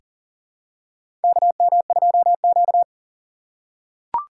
Morse code, heard as a keyed tone, about 40 wpm, 700 Hz. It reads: KM2Q
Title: USA Call Signs - 4 Characters - 6x Character Spacing - 40wpm